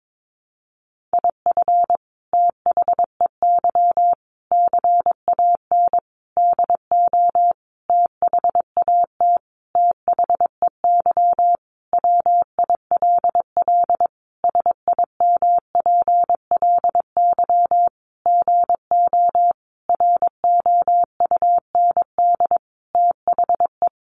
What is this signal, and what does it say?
Morse code, heard as a keyed tone, about 22 wpm, 700 Hz. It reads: IF THEY CAN DO THAT THEY WILL SIMPLY GO ROUND THE
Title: Sentences from Top 600 Words 22wpm